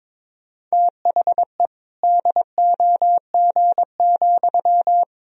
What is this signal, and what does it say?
Morse code, heard as a keyed tone, about 22 wpm, 700 Hz. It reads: THE DOG,
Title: Sentences from Top 1000 Words - Review Part 3 - 22wpm